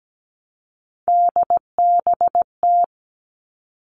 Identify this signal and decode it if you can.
Morse code, heard as a keyed tone, about 17 wpm, 700 Hz. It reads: DBT
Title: All Cut Numbers - 3-Digit 17wpm